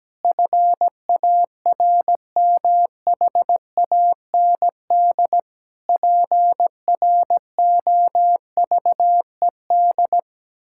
Morse code, heard as a keyed tone, about 17 wpm, 700 Hz. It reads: FARMHAND PROVED
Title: Pangram Challenge v2 - 17wpm